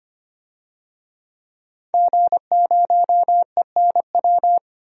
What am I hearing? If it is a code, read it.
Morse code, heard as a keyed tone, about 25 wpm, 700 Hz. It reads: G0ENW
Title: DXCC-100 Call Signs - 5 Characters - 25wpm